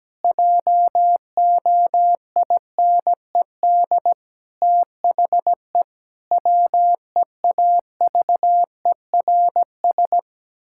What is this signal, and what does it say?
Morse code, heard as a keyed tone, about 17 wpm, 700 Hz. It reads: JOINED THE WEAVERS
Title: Pangram Challenge v2 - 17wpm